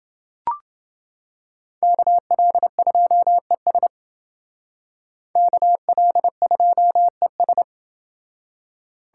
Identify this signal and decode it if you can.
Morse code, heard as a keyed tone, about 28 wpm, 700 Hz. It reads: KL2ES KL2ES